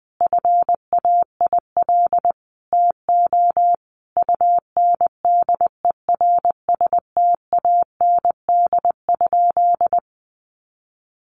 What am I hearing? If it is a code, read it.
Morse code, heard as a keyed tone, about 20 wpm, 700 Hz. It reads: FAIL TO UNDERSTAND?
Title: Sentences from Top 1100 Words 20wpm